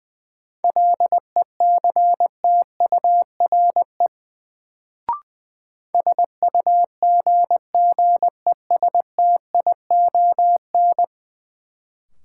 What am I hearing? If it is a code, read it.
Morse code, heard as a keyed tone, about 21 wpm, 700 Hz. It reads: LECTURE SUGGESTION
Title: Sentences from Top 1600 Words - New Word Drill - 20wpm